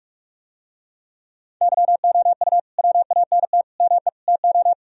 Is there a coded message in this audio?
Yes: Morse code from 1.6 to 4.7 s, heard as a keyed tone, about 49 wpm, 700 Hz. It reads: YOU WANT ME TO